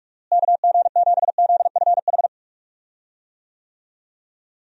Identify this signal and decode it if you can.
Morse code, heard as a keyed tone, about 45 wpm, 700 Hz. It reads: KG7ZFH
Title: USA Call Signs - 6 Characters - Repeated 2x - 45wpm